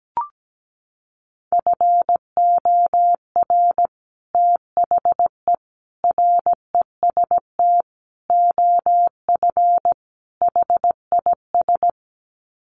Morse code, heard as a keyed tone, about 17 wpm, 700 Hz. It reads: FOR THE REST OF HIS
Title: Sets of 5 Words 17wpm